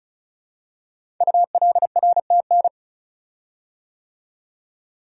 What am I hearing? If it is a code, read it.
Morse code, heard as a keyed tone, about 36 wpm, 700 Hz. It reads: ULRTN
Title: Code Groups - 5 Letters - 35wpm